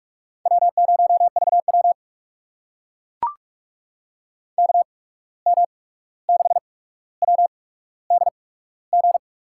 Morse code, heard as a keyed tone, about 44 wpm, spaced as for 17 wpm, 700 Hz. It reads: W0VW KM6WDG
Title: USA Call Signs - 8x Character Spacing - 45wpm